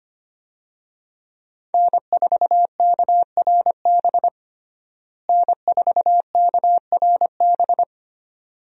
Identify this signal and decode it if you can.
Morse code, heard as a keyed tone, about 25 wpm, 700 Hz. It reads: N4KRB N4KRB
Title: USA Call Signs - 5 Characters - Repeated 2x - 25wpm